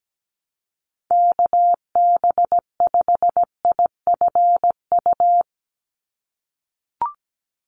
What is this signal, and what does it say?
Morse code, heard as a keyed tone, about 17 wpm, 700 Hz. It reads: KB5IFU
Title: USA Call Signs 17wpm